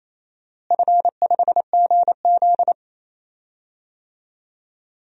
Morse code, heard as a keyed tone, about 28 wpm, 700 Hz. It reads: F5GZ
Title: DXCC-100 Call Signs - 4 Characters - Repeated 2x - 28wpm